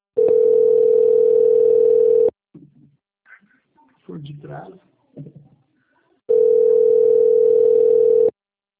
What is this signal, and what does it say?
Ringback tone (North America), 440 + 480 Hz, with speech.
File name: out- - - - - .65958